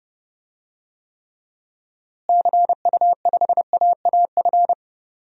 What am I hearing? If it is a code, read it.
Morse code, heard as a keyed tone, about 30 wpm, 700 Hz. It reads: CU5AAF
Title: DXCC-100 Call Signs - 6 Characters - Repeated 3x - 30wpm